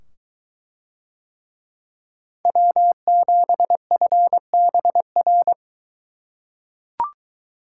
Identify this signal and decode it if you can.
Morse code, heard as a keyed tone, about 21 wpm, 700 Hz. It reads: W7FBR